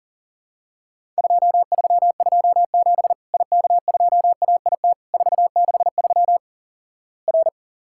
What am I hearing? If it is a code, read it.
Morse code, heard as a keyed tone, about 37 wpm, 700 Hz. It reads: 2327 IK2AIT 463 R
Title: Worked All Europe DX Contest - QTC Exchange - 40wpm